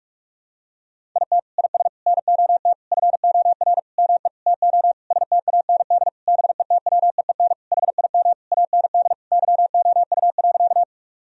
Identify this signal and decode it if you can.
Morse code, heard as a keyed tone, about 46 wpm, 700 Hz. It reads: IT IS NOT FOR ME TO STAND BETWEEN HIM AND YOU.
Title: Sentences from Top 1000 Words - Review Part 1 - 45wpm